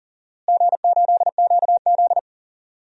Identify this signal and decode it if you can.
Morse code, heard as a keyed tone, about 37 wpm, 700 Hz. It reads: G8QZ
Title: DXCC-100 Call Signs - 4 Characters - Repeated 3x - 40wpm